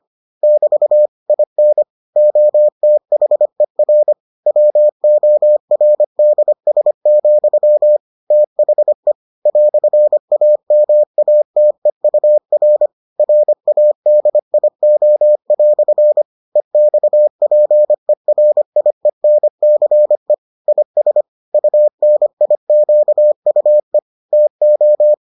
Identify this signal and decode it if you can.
Morse code, heard as a keyed tone, about 25 wpm, 600 Hz. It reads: <BT> IN OTHER WORDS, THE "AMATEUR RADIO" EXPERIENCE IS UNIQUE TO